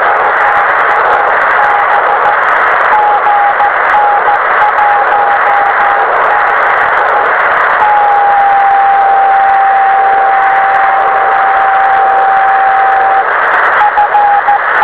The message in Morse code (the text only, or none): none